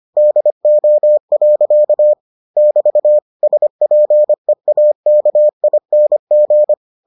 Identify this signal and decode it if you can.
Morse code, heard as a keyed tone, about 25 wpm, 600 Hz. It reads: DO. <BT> SPEAKING